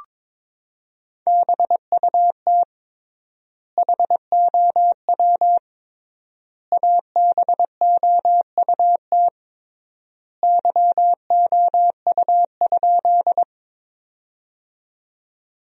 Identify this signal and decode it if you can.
Morse code, heard as a keyed tone, about 22 wpm, 700 Hz. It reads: BUT HOW ABOUT YOU?
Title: 3x Word-Spacing - Sentences from Top 100 Words - 22wpm